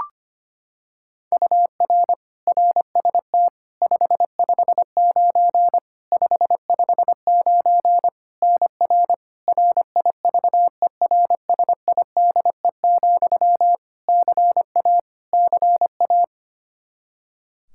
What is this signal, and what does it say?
Morse code, heard as a keyed tone, about 25 wpm, 700 Hz. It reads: UR RST 559 559 NR RIVERSIDE, CA CA
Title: Practice QSO - Exchange 1 - Beginning - 25wpm